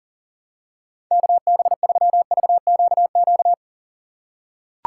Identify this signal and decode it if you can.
Morse code, heard as a keyed tone, about 40 wpm, 700 Hz. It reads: KB3VQQ
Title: USA Call Signs - 6 Characters - 3x Character Spacing - 40wpm